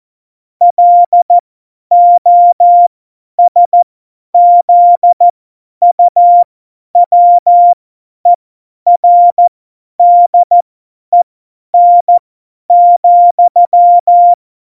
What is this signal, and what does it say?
Morse code, heard as a keyed tone, about 14 wpm, spaced as for 10 wpm, 700 Hz. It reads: LOSZUWERDEN,
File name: Morsekurs / Witze / Witz 11 Juli 17, Tempo 50 F70